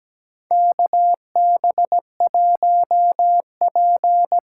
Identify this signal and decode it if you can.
Morse code, heard as a keyed tone, about 17 wpm, 700 Hz. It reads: KB1P
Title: USA Call Signs - 8x Character Spacing - 17wpm